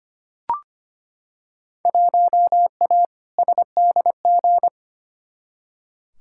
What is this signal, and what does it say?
Morse code, heard as a keyed tone, about 25 wpm, 700 Hz. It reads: 1A SDG
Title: Field Day Exchanges Encore - 25wpm